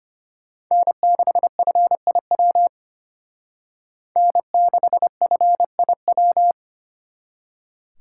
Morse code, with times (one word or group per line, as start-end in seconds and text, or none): 0.7-2.7 N6FIW
4.2-6.5 N6FIW